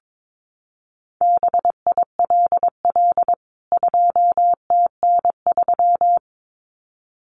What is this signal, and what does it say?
Morse code, heard as a keyed tone, about 22 wpm, 700 Hz. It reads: BILL 2TN3
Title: CWT Exchange 22wpm